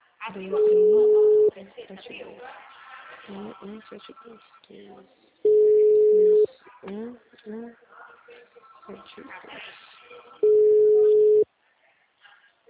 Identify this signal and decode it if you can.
Ringback tone (Europe, 425 Hz), with speech.
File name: out- -2029- - - .67651